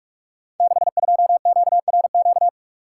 Morse code, heard as a keyed tone, about 44 wpm, 700 Hz. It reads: 62QRQ
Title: Code Groups - 5 Letter:Numbers - 45wpm